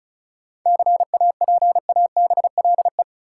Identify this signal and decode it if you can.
Morse code, heard as a keyed tone, about 34 wpm, 700 Hz. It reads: CAPABLE
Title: Sentences from Top 1200 Words - New Word Drill - 35wpm